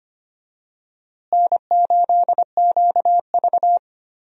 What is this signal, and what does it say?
Morse code, heard as a keyed tone, about 25 wpm, 700 Hz. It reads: N8QV
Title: USA Call Signs - 4 Characters - Repeated 2x - 25wpm